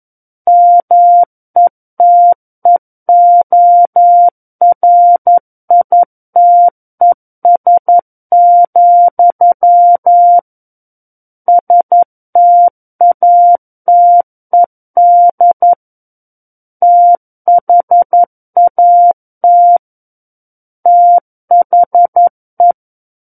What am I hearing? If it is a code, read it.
Morse code, heard as a keyed tone, about 11 wpm, 700 Hz. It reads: METEORITES, STATED THAT THE
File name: Book One / 02-Chapter 2 - 11 WPM